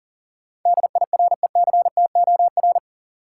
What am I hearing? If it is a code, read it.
Morse code, heard as a keyed tone, about 40 wpm, 700 Hz. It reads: DIRECTOR
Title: Sets of 3 Words - New Word Drill - 40wpm